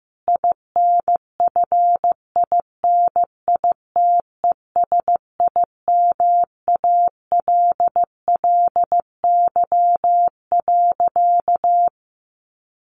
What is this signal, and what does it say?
Morse code, heard as a keyed tone, about 15 wpm, 700 Hz. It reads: INFINITESIMALLY.
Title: A Sound Of Thunder 15wpm